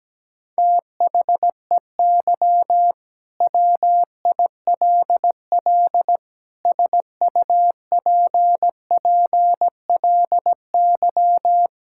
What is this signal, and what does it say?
Morse code, heard as a keyed tone, about 17 wpm, 700 Hz. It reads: THEY WILL SUPPLY
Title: Sentences from Top 1000 Words - Review Part 5 - 17wpm